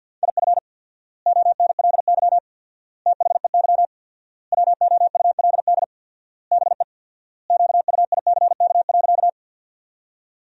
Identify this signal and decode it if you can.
Morse code, heard as a keyed tone, about 53 wpm, 700 Hz. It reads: IF ONLY THEY WOULD BE QUICK.
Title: 4x Word-Spacing - Sentences from Top 500 Words - 50wpm